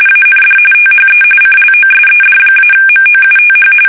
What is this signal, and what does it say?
RTTY, about 45.45 baud. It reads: AND THE CLOSED-MOUTH
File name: fsk850